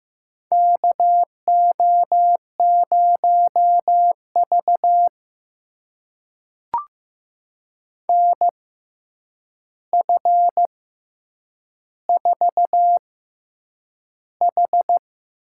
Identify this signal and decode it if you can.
Morse code, heard as a keyed tone, about 15 wpm, spaced as for 8 wpm, 700 Hz. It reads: KO0V NF4H
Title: USA Call Signs - 4 Characters - 6x Character Spacing - 15wpm